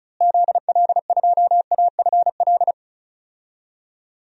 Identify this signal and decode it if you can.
Morse code, heard as a keyed tone, about 36 wpm, 700 Hz. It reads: ZL2AFL
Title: DXCC-100 Call Signs - 6 Characters - Repeated 2x - 35wpm